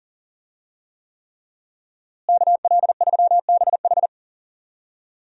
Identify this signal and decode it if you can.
Morse code, heard as a keyed tone, about 37 wpm, 700 Hz. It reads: KL3BH
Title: USA Call Signs - 5 Characters - Repeated 3x - 40wpm